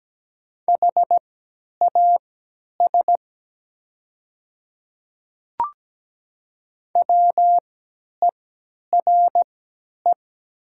Morse code, heard as a keyed tone, about 17 wpm, spaced as for 10 wpm, 700 Hz. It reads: HAS WERE
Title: 3x Character Spacing - Top 100 Words - 17wpm